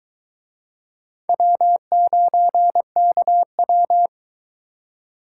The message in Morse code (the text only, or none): W9KW